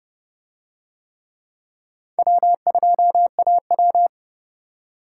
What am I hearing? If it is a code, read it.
Morse code, heard as a keyed tone, about 28 wpm, 700 Hz. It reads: W2AW